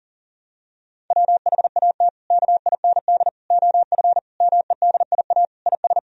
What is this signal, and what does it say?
Morse code, heard as a keyed tone, about 40 wpm, 700 Hz. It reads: WHAT KIND OF MEDIA IS
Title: Sentences from Top 1000 Words - Review Part 4 - 40wpm